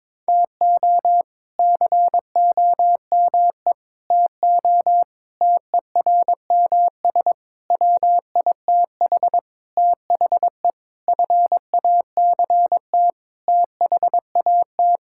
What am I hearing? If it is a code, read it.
Morse code, heard as a keyed tone, about 22 wpm, 700 Hz. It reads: TO COME TO TERMS WITH THE FACT THAT
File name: practice / 22wpm / Sentences with 8-Word Phrases - 22wpm